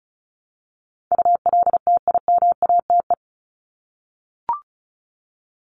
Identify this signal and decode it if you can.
Morse code, heard as a keyed tone, about 34 wpm, 700 Hz. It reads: ULTIMATE
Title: LY Suffix - Top 100 Words with Base Words 35wpm